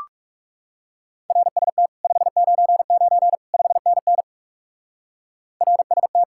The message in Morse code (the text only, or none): RST 599 5NN RST